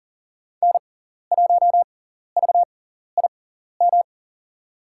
Morse code, heard as a keyed tone, about 40 wpm, spaced as for 14 wpm, 700 Hz. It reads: N1VIM
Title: USA Call Signs - 5 Characters - 6x Character Spacing - 40wpm